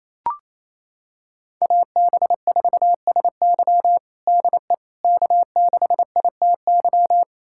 Morse code, heard as a keyed tone, about 28 wpm, 700 Hz. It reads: AB4SY DE K6ITY
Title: USA Callsign DE Callsign - 5 Characters - 28wpm